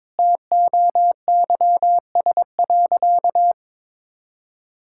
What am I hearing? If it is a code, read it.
Morse code, heard as a keyed tone, about 22 wpm, 700 Hz. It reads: TOYS.
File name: practice / 22wpm / Sentences with 2-Word Phrases - 22wpm